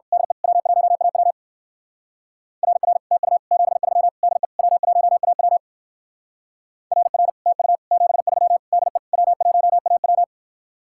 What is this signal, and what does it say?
Morse code, heard as a keyed tone, about 52 wpm, 700 Hz. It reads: DE W1AW RR TU 73 DE W1AW RR TU 73 DE W1AW
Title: Speed-Racing Common QSO Phrases Encore - 45wpm